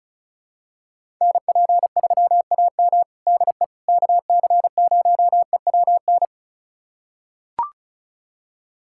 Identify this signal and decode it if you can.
Morse code, heard as a keyed tone, about 36 wpm, 700 Hz. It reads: NP3AM DE KC0EWN
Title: USA Callsign DE Callsign 35wpm